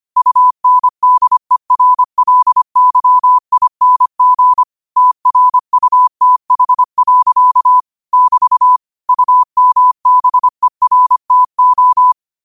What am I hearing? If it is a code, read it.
Morse code, heard as a keyed tone, about 25 wpm, 1000 Hz. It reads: UNDERLYING TRUTH. <BT> UMBERTO
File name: 25wpm